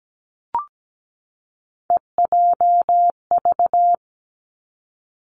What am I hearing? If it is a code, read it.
Morse code, heard as a keyed tone, about 17 wpm, 700 Hz. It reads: EJV
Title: Three Challenging Letter-Number Combinations 17wpm